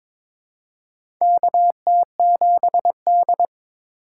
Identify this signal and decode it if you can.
Morse code, heard as a keyed tone, about 22 wpm, 700 Hz. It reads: KT7D
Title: USA Call Signs - 4 Characters - 3x Character Spacing - 22wpm